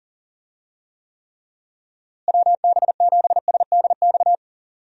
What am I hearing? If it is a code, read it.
Morse code, heard as a keyed tone, about 40 wpm, 700 Hz. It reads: WB7SDX
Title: USA Call Signs - 6 Characters - Repeated 3x - 40wpm